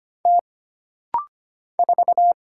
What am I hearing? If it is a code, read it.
Morse code, heard as a keyed tone, about 25 wpm, 700 Hz. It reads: T 4
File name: Single Character (Letter:Number:Punctuation:Prosign) - Rapid-Fire - 25wpm